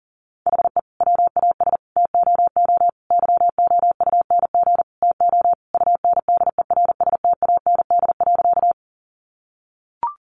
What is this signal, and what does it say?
Morse code, heard as a keyed tone, about 40 wpm, 700 Hz. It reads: HE WAS TOO YOUNG TO UNDERSTAND.
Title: Sentences from Top 600 Words 40wpm